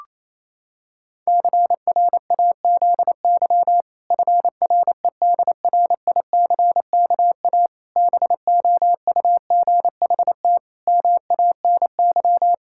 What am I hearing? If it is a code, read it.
Morse code, heard as a keyed tone, about 28 wpm, 700 Hz. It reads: CRAZY FREDRICKA BOUGHT MANY
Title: Pangram Challenge v2 - 28wpm